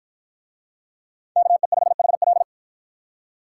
Morse code, heard as a keyed tone, about 53 wpm, 700 Hz. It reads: KE5HL